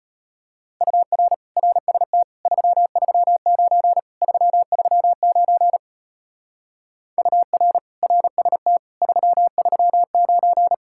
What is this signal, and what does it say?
Morse code, heard as a keyed tone, about 32 wpm, 700 Hz. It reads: UR RST 339 339 UR RST 339